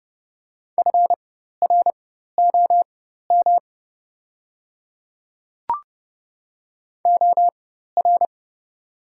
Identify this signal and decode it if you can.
Morse code, heard as a keyed tone, about 30 wpm, spaced as for 14 wpm, 700 Hz. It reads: FROM OR
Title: 4x Character Spacing - Top 200 Words - 30wpm